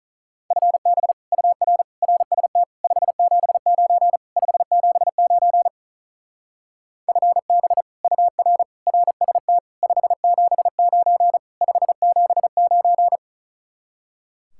Morse code, heard as a keyed tone, about 44 wpm, 700 Hz. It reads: FB UR RST 579 579 FB UR RST 579 579